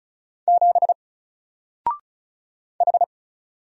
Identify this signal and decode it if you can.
Morse code, heard as a keyed tone, about 34 wpm, 700 Hz. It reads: 7 H
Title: Single Character (Letter:Number:Punctuation:Prosign) Alt 35wpm